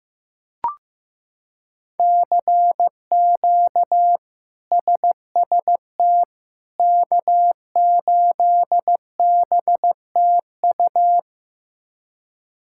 Morse code, heard as a keyed tone, about 15 wpm, 700 Hz. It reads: CQ SST K8BTU
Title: K1USN SST QSO Practice - 15wpm